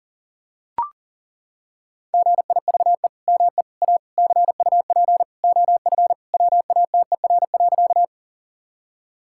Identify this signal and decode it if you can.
Morse code, heard as a keyed tone, about 40 wpm, 700 Hz. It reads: GIVE ME A CUP OF WATER.
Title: Sentences from Top 1600 Words 40wpm